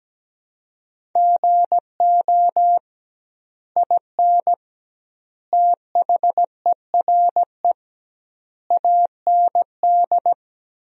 Morse code, heard as a keyed tone, about 17 wpm, 700 Hz. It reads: GO IN THERE AND
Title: 2x Word-Spacing - Sentences from Top 500 Words - 17wpm